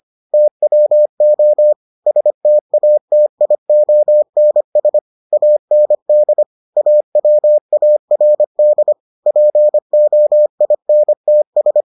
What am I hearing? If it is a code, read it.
Morse code, heard as a keyed tone, about 25 wpm, 600 Hz. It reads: TWO STATIONS AND AWARD POINTS